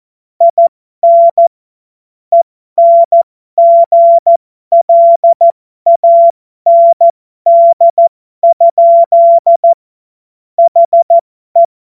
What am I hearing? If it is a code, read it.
Morse code, heard as a keyed tone, about 14 wpm, spaced as for 12 wpm, 680 Hz. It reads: IN ENGLAND? HE
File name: Morsekurs / Witze